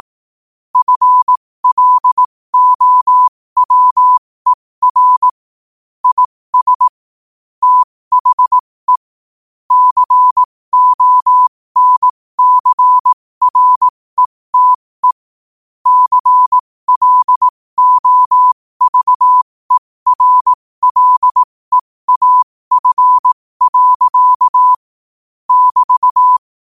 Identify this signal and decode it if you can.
Morse code, heard as a keyed tone, about 18 wpm, spaced as for 15 wpm, 1000 Hz. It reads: FLOWER IS THE CONCRETE CLOVERLEAF. <BT>
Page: New quotes every day in morse code at 15 Words per minute.